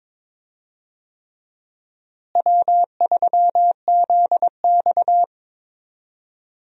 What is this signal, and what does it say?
Morse code, heard as a keyed tone, about 22 wpm, 700 Hz. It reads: W3ZX
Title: USA Call Signs - 4 Characters - Repeated 3x - 22wpm